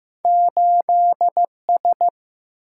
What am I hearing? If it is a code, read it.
Morse code, heard as a keyed tone, about 15 wpm, 700 Hz. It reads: 8S
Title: Single-Letter Single-Number 15wpm